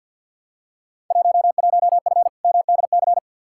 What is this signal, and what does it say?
Morse code, heard as a keyed tone, about 47 wpm, 700 Hz. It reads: 11F MDC
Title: Field Day Exchanges Encore - 50wpm